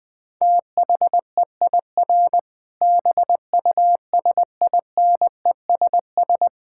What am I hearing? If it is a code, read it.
Morse code, heard as a keyed tone, about 20 wpm, 700 Hz. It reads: THEIR BUSINESS
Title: Sentences from Top 1000 Words - Review Part 5 - 20wpm